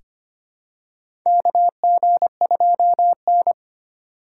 Morse code, heard as a keyed tone, about 25 wpm, 700 Hz. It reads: KG2N
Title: USA Call Signs - 4 Characters - 6x Character Spacing - 25wpm